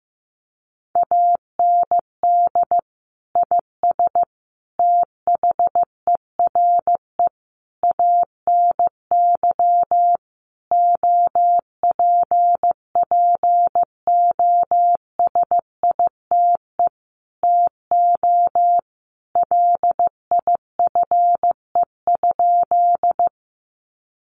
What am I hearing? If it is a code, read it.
Morse code, heard as a keyed tone, about 14 wpm, 700 Hz. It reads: AND IS THERE ANY OPPOSITE TO LIFE?
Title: Sentences from Top 1300 Words 15wpm